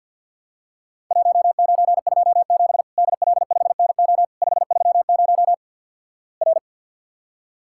Morse code, heard as a keyed tone, about 52 wpm, 700 Hz. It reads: 1927 DL5NO 530 R
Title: Worked All Europe DX Contest - QTC Exchange - 50wpm